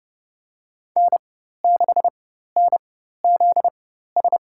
Morse code, heard as a keyed tone, about 30 wpm, spaced as for 14 wpm, 700 Hz. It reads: N6NZS
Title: USA Call Signs - 4x Character Spacing - 30wpm